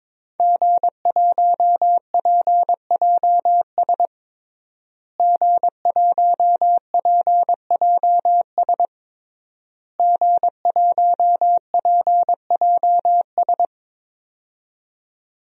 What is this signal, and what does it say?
Morse code, heard as a keyed tone, about 22 wpm, 700 Hz. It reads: G1PJS G1PJS G1PJS
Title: DXCC-100 Call Signs - 5 Characters - Repeated 3x - 22wpm